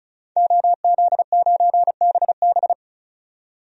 Morse code, heard as a keyed tone, about 36 wpm, 700 Hz. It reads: OZ9BB
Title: DXCC-100 Call Signs - 5 Characters - 35wpm